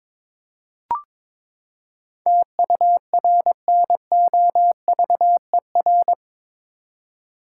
Morse code, heard as a keyed tone, about 22 wpm, 700 Hz. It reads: TURNOVER
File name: practice / 22wpm / Eight Letter Words - 22wpm